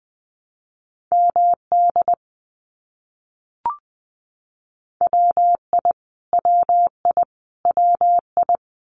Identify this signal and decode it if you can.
Morse code, heard as a keyed tone, about 20 wpm, 700 Hz. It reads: MD WI WI WI
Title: US States Repeated 3x 20wpm